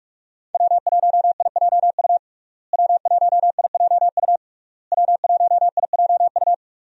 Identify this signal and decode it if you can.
Morse code, heard as a keyed tone, about 42 wpm, 700 Hz. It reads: W1IJU W1IJU W1IJU
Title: USA Call Signs - 5 Characters - Repeated 3x - 45wpm